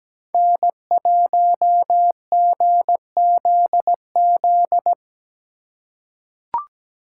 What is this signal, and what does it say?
Morse code, heard as a keyed tone, about 17 wpm, 700 Hz. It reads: N1GZZ
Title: USA Call Signs - 5 Characters - 4x Character Spacing - 17wpm